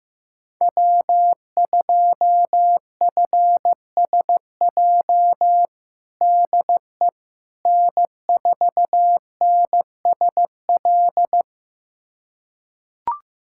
Morse code, heard as a keyed tone, about 15 wpm, 700 Hz. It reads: W2FSJ DE N4NSL
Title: USA Callsign DE Callsign - 5 Characters - 15wpm